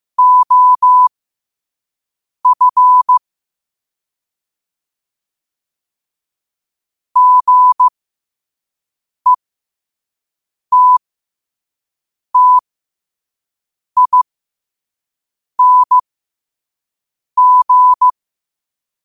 Morse code, heard as a keyed tone, about 15 wpm, spaced as for 5 wpm, 1000 Hz. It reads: OF GETTING
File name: qotdfiles / 5wpm